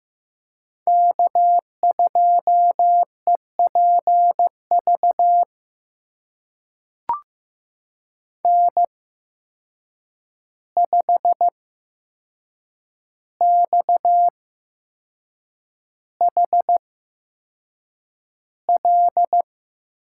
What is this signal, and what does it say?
Morse code, heard as a keyed tone, about 15 wpm, spaced as for 6 wpm, 700 Hz. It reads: K2EPV N5XHL
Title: USA Call Signs - 8x Character Spacing - 15wpm